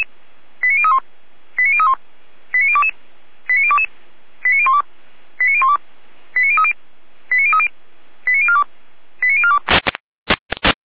Five-tone selective call (ZVEI), heard as quick runs of five tones, tones of about 70 ms. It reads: 89031 89031 89022 89022 89012 89012 89030 89030 89042 89042